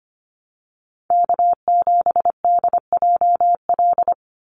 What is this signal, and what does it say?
Morse code, heard as a keyed tone, about 25 wpm, 700 Hz. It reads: K7DJL
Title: NA SOTA Chaser Callsigns 25wpm